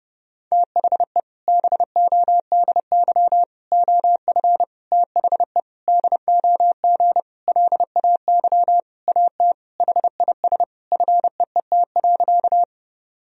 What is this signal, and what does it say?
Morse code, heard as a keyed tone, about 30 wpm, 700 Hz. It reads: THE BODY OF THE DOG LAY AT HIS FEET.
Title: Sentences from Top 1000 Words - Review Part 2 - 30wpm